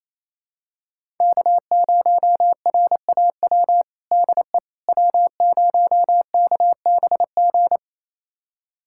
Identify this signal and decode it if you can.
Morse code, heard as a keyed tone, about 28 wpm, 700 Hz. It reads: K0RAW DE W0KBG
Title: USA Callsign DE Callsign - 5 Characters - 28wpm